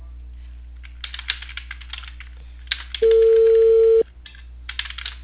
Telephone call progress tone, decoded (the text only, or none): ringback tone